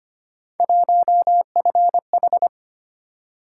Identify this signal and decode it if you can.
Morse code, heard as a keyed tone, about 29 wpm, 700 Hz. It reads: 1FH
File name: Speed-Racing - Three Letter-Number Combinations - 25wpm